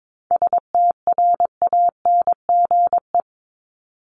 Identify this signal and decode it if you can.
Morse code, heard as a keyed tone, about 22 wpm, 700 Hz. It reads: STRANGE
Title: Top 1501-1750 Words 22wpm